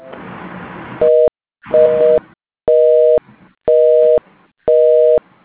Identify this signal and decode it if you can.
Busy tone (North America), 480 + 620 Hz.